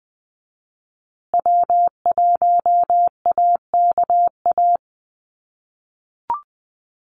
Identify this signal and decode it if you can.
Morse code, heard as a keyed tone, about 20 wpm, 700 Hz. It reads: W1AKA
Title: USA Call Signs Repeated 3x 20wpm